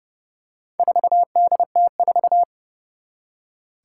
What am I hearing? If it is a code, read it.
Morse code, heard as a keyed tone, about 31 wpm, 700 Hz. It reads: 4DT4
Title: Four Challenging Letter-Number Combinations 30wpm